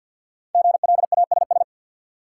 Morse code, heard as a keyed tone, about 53 wpm, 700 Hz. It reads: GLASS
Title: Sentences from Top 1000 Words - Review Part 3 - New Word Drill - 50wpm